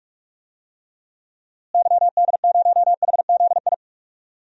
Morse code, heard as a keyed tone, about 41 wpm, 700 Hz. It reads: YD0HZI
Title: DXCC-100 Call Signs - 6 Characters - Repeated 3x - 45wpm